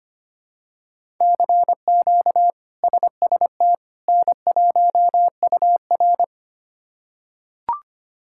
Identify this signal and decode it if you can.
Morse code, heard as a keyed tone, about 25 wpm, 700 Hz. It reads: CQ SST N1UR
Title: K1USN SST QSO Practice - 25wpm